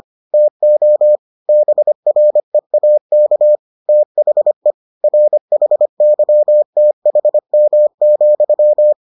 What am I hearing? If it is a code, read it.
Morse code, heard as a keyed tone, about 25 wpm, 600 Hz. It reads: TO BREAK THE RHYTHM,